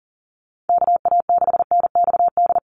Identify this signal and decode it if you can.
Morse code, heard as a keyed tone, about 40 wpm, 700 Hz. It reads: KA6NXD
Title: USA Call Signs 40wpm